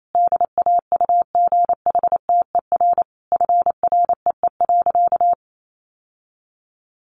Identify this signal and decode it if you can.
Morse code, heard as a keyed tone, about 28 wpm, 700 Hz. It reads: DAUGHTER FREE.
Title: Sentences from Top 1500 Words 28wpm